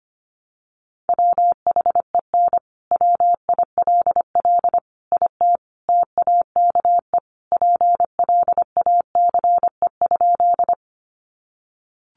Morse code, heard as a keyed tone, about 25 wpm, 700 Hz. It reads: WHEN WILL IT TAKE PLACE?
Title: Sentences from Top 200 Words 25wpm